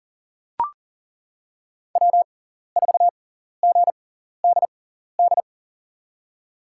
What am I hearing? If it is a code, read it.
Morse code, heard as a keyed tone, about 40 wpm, spaced as for 14 wpm, 700 Hz. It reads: W4GDD
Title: USA Call Signs - 5 Characters - 6x Character Spacing - 40wpm